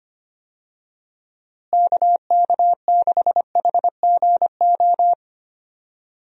Morse code, heard as a keyed tone, about 25 wpm, 700 Hz. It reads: KK6HGO
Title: USA Call Signs - 6 Characters - Repeated 2x - 25wpm